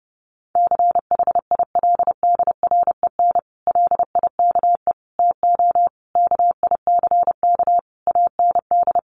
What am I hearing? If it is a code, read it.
Morse code, heard as a keyed tone, about 30 wpm, 700 Hz. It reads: CHILDREN LIKE TO KICK AND
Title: Sentences from Top 1300 Words 30wpm